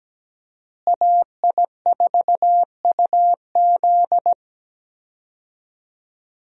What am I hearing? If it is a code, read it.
Morse code, heard as a keyed tone, about 17 wpm, 700 Hz. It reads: AI4UZ
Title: USA Call Signs - 5 Characters - Repeated 3x - 17wpm